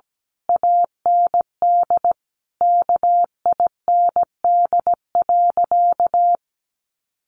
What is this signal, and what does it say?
Morse code, heard as a keyed tone, about 17 wpm, 700 Hz. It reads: AND KIND.
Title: Sentences from Top 1500 Words 17wpm